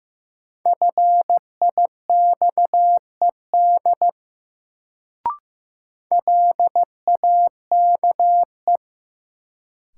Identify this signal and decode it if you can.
Morse code, heard as a keyed tone, about 15 wpm, 700 Hz. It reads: FIXED LAKE
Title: Sentences from Top 1100 Words - New Word Drill Alt - 15wpm